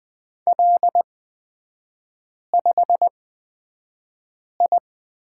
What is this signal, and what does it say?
Morse code, heard as a keyed tone, about 20 wpm, spaced as for 5 wpm, 700 Hz. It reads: L5I
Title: 224 Lesson 36e - BT - Three Characters 20wpm